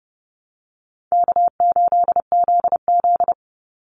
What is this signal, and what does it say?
Morse code, heard as a keyed tone, about 30 wpm, 700 Hz. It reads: K8ZZ
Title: Call Signs Repeated 3x - 30wpm